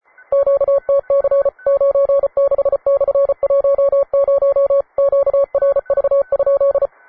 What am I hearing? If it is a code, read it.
Morse code, heard as a keyed tone, about 34 wpm, 560 Hz. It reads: QTC 96/10 QRV?
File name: wae / cw advanced / un4l cw